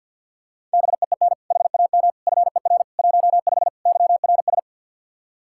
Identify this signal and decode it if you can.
Morse code, heard as a keyed tone, about 46 wpm, 700 Hz. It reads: BEEN HAM FER 15 YRS
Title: Practice QSO - Exchange 3 - More Info - 50wpm